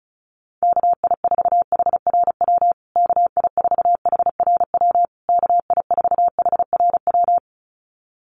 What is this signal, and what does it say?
Morse code, heard as a keyed tone, about 34 wpm, 700 Hz. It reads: KI4HRW KI4HRW KI4HRW
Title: USA Call Signs Repeated 3x 35wpm